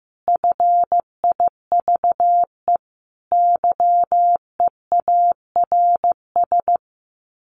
Morse code, heard as a keyed tone, about 15 wpm, 700 Hz. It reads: FIVE YEARS
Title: Sets of 5 Words Encore 15wpm